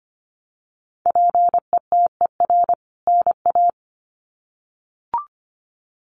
Morse code, heard as a keyed tone, about 25 wpm, 700 Hz. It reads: PETER NA
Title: CWT Exchange 25wpm